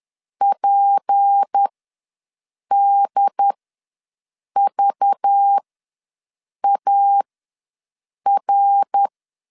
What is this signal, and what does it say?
Morse code, heard as a keyed tone, about 10 wpm, spaced as for 6 wpm, 800 Hz. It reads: PDVAR
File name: LEZIONI / 6 PKX0 / 6 P ws 10wpm 3751